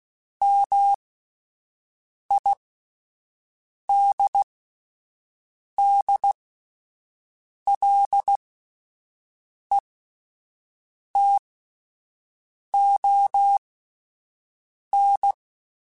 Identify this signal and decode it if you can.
Morse code, heard as a keyed tone, about 16 wpm, spaced as for 5 wpm, 800 Hz. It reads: MIDDLETON